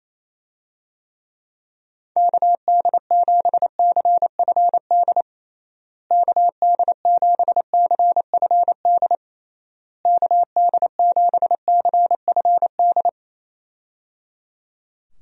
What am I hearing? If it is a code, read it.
Morse code, heard as a keyed tone, about 28 wpm, 700 Hz. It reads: KD7CFD KD7CFD KD7CFD
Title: USA Call Signs - 6 Characters - Repeated 3x - 28wpm